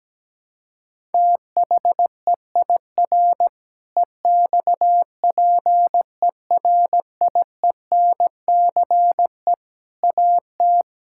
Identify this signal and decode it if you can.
Morse code, heard as a keyed tone, about 17 wpm, 700 Hz. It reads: THEIR EXPERIENCE AT
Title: Sentences from Top 1000 Words - Review Part 6 - 17wpm